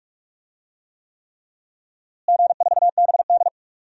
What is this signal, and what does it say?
Morse code, heard as a keyed tone, about 42 wpm, 700 Hz. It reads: G4BD
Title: DXCC-100 Call Signs - 4 Characters - Repeated 3x - 45wpm